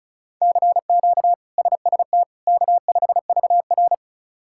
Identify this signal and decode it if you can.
Morse code, heard as a keyed tone, about 36 wpm, 700 Hz. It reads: CQ SST K5VR
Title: K1USN SST QSO Practice - 35wpm